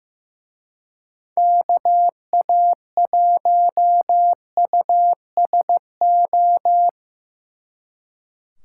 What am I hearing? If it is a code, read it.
Morse code, heard as a keyed tone, about 17 wpm, 700 Hz. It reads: KA1USO
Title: Speed Racing - USA Call Signs 15wpm